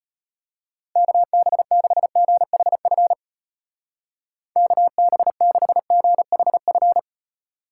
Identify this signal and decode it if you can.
Morse code, heard as a keyed tone, about 31 wpm, 700 Hz. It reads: KB6GHF KB6GHF
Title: Speed Racing - USA 2x3 Call Signs 25wpm